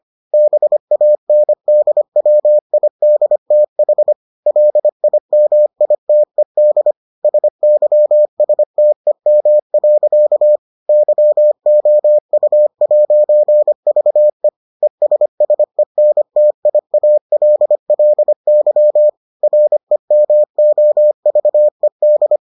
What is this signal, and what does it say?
Morse code, heard as a keyed tone, about 25 wpm, 600 Hz. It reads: BANDWIDTH LIMITED SYSTEM. YOU'VE ESSENTIALLY REMOVED